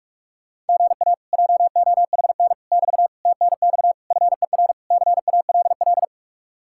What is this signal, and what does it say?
Morse code, heard as a keyed tone, about 59 wpm, spaced as for 51 wpm, 700 Hz. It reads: GA JOHN <BT> TNX FER CALL
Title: Speed-Racing Common QSO Phrases - 45wpm